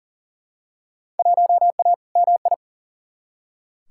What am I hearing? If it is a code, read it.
Morse code, heard as a keyed tone, about 40 wpm, 700 Hz. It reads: 1A MI
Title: Field Day Exchanges - 40wpm